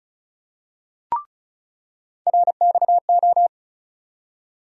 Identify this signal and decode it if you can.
Morse code, heard as a keyed tone, about 36 wpm, 700 Hz. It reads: RXO
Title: USA Call Sign Postfixes - 3 Letters 35wpm